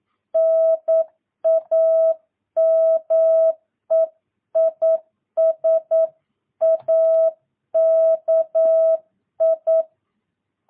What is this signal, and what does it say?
Morse code, heard as a keyed tone, about 9 wpm, 650 Hz. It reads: NAME IS AKI
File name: jh8cbh / 2022cw